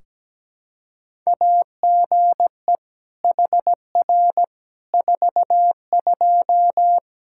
Morse code, heard as a keyed tone, about 17 wpm, 700 Hz. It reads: AGE HR 42
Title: Practice QSO - Exchange 3 - More Info - 17wpm